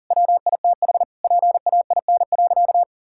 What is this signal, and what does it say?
Morse code, heard as a keyed tone, about 40 wpm, 700 Hz. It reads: WITH PAIN.
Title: Sentences from Top 1600 Words 40wpm